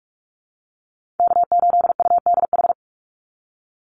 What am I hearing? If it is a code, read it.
Morse code, heard as a keyed tone, about 44 wpm, 700 Hz. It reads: K8UDH
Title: Call Signs - 45wpm